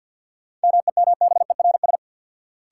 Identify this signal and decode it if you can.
Morse code, heard as a keyed tone, about 53 wpm, 700 Hz. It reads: MEMBERS
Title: Sets of 3 Words - New Word Drill - 50wpm